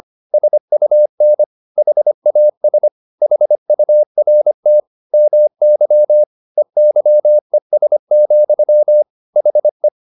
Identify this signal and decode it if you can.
Morse code, heard as a keyed tone, about 25 wpm, 600 Hz. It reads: SUN HAS HURT MY EYES, HE
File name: the old man and the sea part 1 25wpm